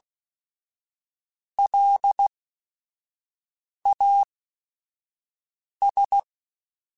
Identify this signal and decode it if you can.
Morse code, heard as a keyed tone, about 16 wpm, spaced as for 5 wpm, 800 Hz. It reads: LAS